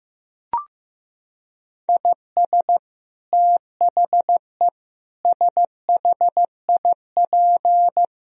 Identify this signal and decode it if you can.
Morse code, heard as a keyed tone, about 16 wpm, 700 Hz. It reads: IS THE SHIP
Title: Sentences from Top 1000 Words - Review Part 2 - 15wpm